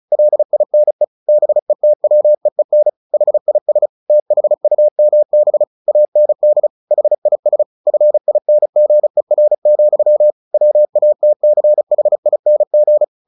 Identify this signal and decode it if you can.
Morse code, heard as a keyed tone, about 34 wpm, 600 Hz. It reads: LINE BETWEEN HIS THUMB AND HIS FINGER, WATCHING
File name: the old man and the sea / the old man and the sea 1 35wpm